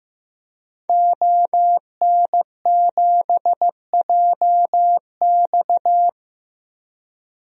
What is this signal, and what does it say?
Morse code, heard as a keyed tone, about 15 wpm, 700 Hz. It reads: ON7JX
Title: DXCC-100 Call Signs - 5 Characters - Repeated 3x - 15wpm